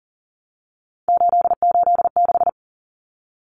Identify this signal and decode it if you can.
Morse code, heard as a keyed tone, about 40 wpm, 700 Hz. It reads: 886
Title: Numbers - 3-Digit 40wpm